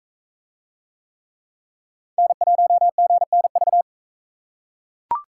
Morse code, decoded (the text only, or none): N1GNV